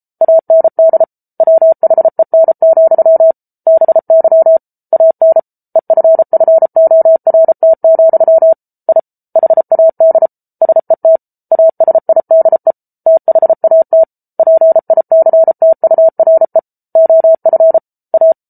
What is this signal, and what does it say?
Morse code, heard as a keyed tone, about 34 wpm, 650 Hz. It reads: AND WHEN, BY AN EFFORT, I HAD SET ASIDE THAT PICTURE OF A